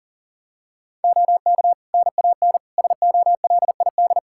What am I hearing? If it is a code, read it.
Morse code, heard as a keyed tone, about 40 wpm, 700 Hz. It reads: OK NAN SOLID
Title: Practice QSO - Exchange 3 - More Info - 40wpm